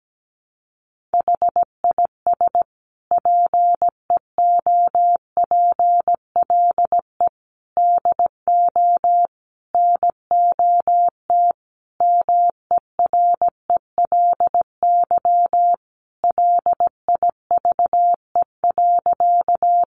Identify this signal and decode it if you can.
Morse code, heard as a keyed tone, about 17 wpm, 700 Hz. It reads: HIS PEOPLE DO NOT MERELY LIVE.
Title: Sentences from Top 1400 Words 17wpm